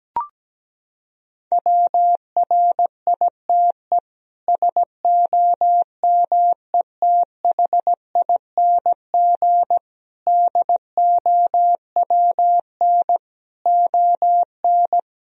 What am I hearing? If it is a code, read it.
Morse code, heard as a keyed tone, about 17 wpm, 700 Hz. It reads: WRITE SOMETHING DOWN ON
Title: Sentences from Top 1000 Words - Review Part 1 - 17wpm